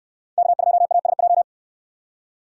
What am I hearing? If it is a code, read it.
Morse code, heard as a keyed tone, about 66 wpm, 700 Hz. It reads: K2ASJ